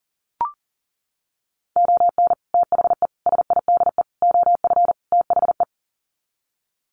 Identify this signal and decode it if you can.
Morse code, heard as a keyed tone, about 40 wpm, 700 Hz. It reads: ON THE SIDE OF THE
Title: Sets of 5 Words 40wpm